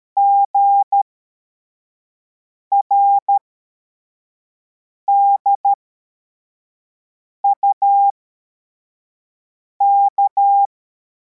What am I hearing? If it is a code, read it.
Morse code, heard as a keyed tone, about 13 wpm, spaced as for 4 wpm, 800 Hz. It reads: GRDUK
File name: formation / cw / Text43